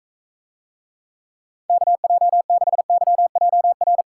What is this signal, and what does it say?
Morse code, heard as a keyed tone, about 44 wpm, 700 Hz. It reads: KJ6YJR